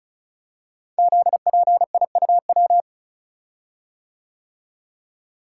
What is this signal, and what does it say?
Morse code, heard as a keyed tone, about 36 wpm, 700 Hz. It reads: ZPIUW
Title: Code Groups - 5 Letters - 35wpm